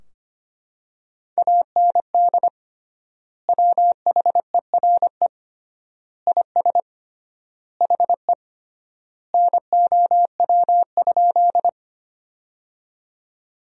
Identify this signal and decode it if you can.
Morse code, heard as a keyed tone, about 25 wpm, 700 Hz. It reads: AND WHERE IS HE NOW?
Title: 3x Word-Spacing - Sentences from Top 200 Words - 25wpm